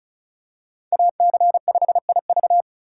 Morse code, heard as a keyed tone, about 36 wpm, 700 Hz. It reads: AC5IV
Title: USA Call Signs - 5 Characters - 2x Character Spacing - 35wpm